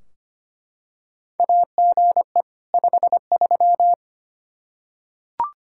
Morse code, heard as a keyed tone, about 25 wpm, 700 Hz. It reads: AGE 53
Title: Practice QSO - Exchange 3 - More Info - 25wpm